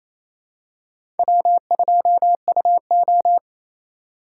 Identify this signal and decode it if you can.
Morse code, heard as a keyed tone, about 27 wpm, 700 Hz. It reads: W2UO
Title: USA Call Signs - 4x Character Spacing - 28wpm